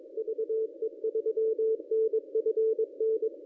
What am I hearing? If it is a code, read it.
Morse code, heard as a keyed tone, about 22 wpm, 440 Hz. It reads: VE3NFN
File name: contests / ss-cw-k5tr-op-n5rz -1000